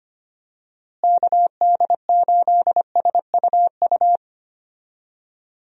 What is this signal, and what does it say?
Morse code, heard as a keyed tone, about 25 wpm, 700 Hz. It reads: KD8SUU
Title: USA Call Signs - 6 Characters - Repeated 2x - 25wpm